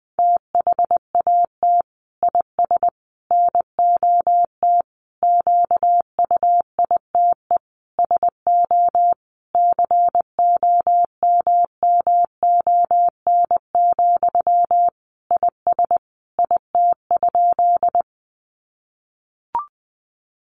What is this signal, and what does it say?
Morse code, heard as a keyed tone, about 20 wpm, 700 Hz. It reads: THAT IS NOT QUITE SO COMMON, IS IT?
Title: Sentences from Top 700 Words 20wpm